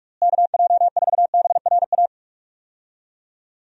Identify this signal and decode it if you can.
Morse code, heard as a keyed tone, about 41 wpm, 700 Hz. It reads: KJ4BRA
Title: USA Call Signs - 6 Characters - Repeated 3x - 45wpm